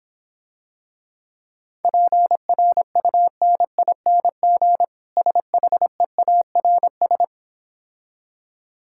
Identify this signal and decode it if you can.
Morse code, heard as a keyed tone, about 26 wpm, 700 Hz. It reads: PRUNING SHEARS